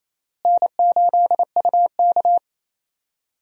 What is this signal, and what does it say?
Morse code, heard as a keyed tone, about 28 wpm, 700 Hz. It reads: N8UK
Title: USA Call Signs - 4 Characters - Repeated 3x - 28wpm